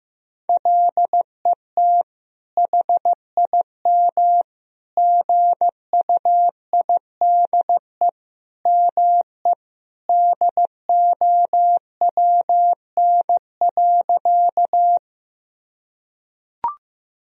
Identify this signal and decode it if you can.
Morse code, heard as a keyed tone, about 15 wpm, 700 Hz. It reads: LET HIM GUIDE ME DOWN.
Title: Sentences from Top 1000 Words - Review Part 3 - 15wpm